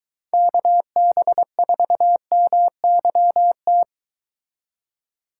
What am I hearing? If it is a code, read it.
Morse code, heard as a keyed tone, about 22 wpm, 700 Hz. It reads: KB4MYT